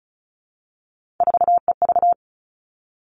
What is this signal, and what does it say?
Morse code, heard as a keyed tone, about 36 wpm, 700 Hz. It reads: 4EV
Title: Three Challenging Letter-Number Combinations 35wpm